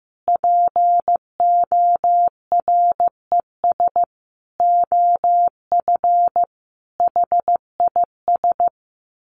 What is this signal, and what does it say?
Morse code, heard as a keyed tone, about 15 wpm, 700 Hz. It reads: PORES OF HIS
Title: A Sound Of Thunder 15wpm